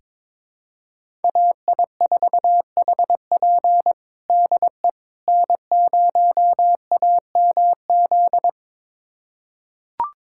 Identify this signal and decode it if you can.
Morse code, heard as a keyed tone, about 22 wpm, 700 Hz. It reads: AI4HP DE N0AMZ
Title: USA Callsign DE Callsign - 5 Characters - 22wpm